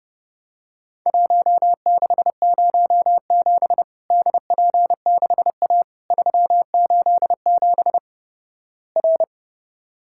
Morse code, heard as a keyed tone, about 29 wpm, 700 Hz. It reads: 1607 DP6A 387 R
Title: Worked All Europe DX Contest - QTC Exchange - 30wpm